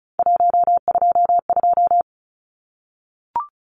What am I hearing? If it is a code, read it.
Morse code, heard as a keyed tone, about 34 wpm, 700 Hz. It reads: 122
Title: Numbers - 3-Digit 35wpm